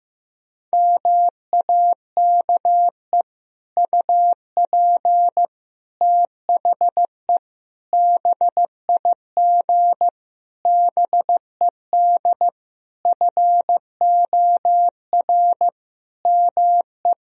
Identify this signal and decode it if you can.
Morse code, heard as a keyed tone, about 15 wpm, 700 Hz. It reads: MAKE UP THE BIG BED FOR ME
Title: Sentences from Top 1000 Words - Review Part 1 - 15wpm